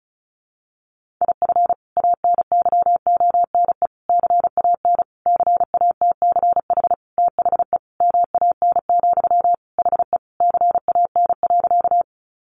Morse code, heard as a keyed tone, about 34 wpm, 700 Hz. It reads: IF ANYONE CAN CATCH THE MAN, HE CAN.
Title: Sentences from Top 1500 Words 35wpm